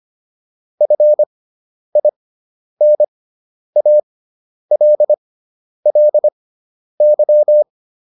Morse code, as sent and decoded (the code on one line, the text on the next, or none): ..-. .. -. .- .-.. .-.. -.--
FINALLY